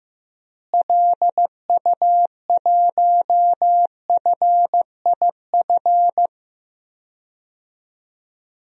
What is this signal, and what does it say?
Morse code, heard as a keyed tone, about 15 wpm, 700 Hz. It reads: LU1FIF
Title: DXCC-100 Call Signs - 6 Characters - Repeated 2x - 15wpm